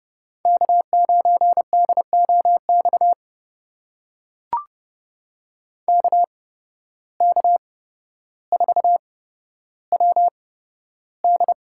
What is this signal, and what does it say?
Morse code, heard as a keyed tone, about 30 wpm, spaced as for 13 wpm, 700 Hz. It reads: K9DOX KK4WD
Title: USA Call Signs - 5 Characters - 8x Character Spacing - 30wpm